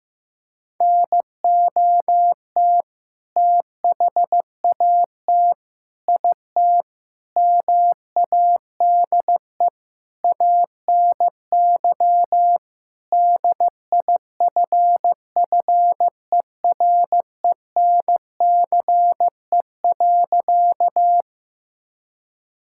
Morse code, heard as a keyed tone, about 15 wpm, 700 Hz. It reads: NOT THAT IT MADE ANY DIFFERENCE.
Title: Sentences from Top 1000 Words - Review Part 5 - 15wpm